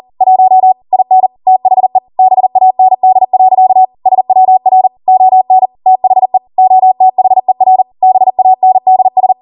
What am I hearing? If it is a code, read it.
Morse code, heard as a keyed tone, about 40 wpm, 750 Hz. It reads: 1 IN THE BAND. SWR ON THE OTHER BANDS